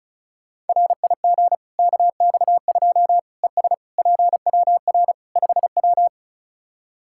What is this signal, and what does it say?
Morse code, heard as a keyed tone, about 36 wpm, 700 Hz. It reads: RIG KX2 ES PWR 5W
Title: Practice QSO - Exchange 2 - Info - 35wpm